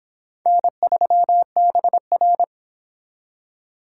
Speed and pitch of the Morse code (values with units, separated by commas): 26 wpm, 700 Hz